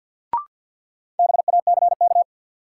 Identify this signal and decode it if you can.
Morse code, heard as a keyed tone, about 52 wpm, 700 Hz. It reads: BACK
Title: Sets of 2 Words - New Word Drill - 50wpm